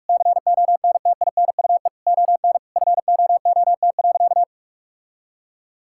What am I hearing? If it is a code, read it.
Morse code, heard as a keyed tone, about 46 wpm, 700 Hz. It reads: CONTINUE ON FOOT.
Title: Sentences from Top 1000 Words - Review Part 6 - 45wpm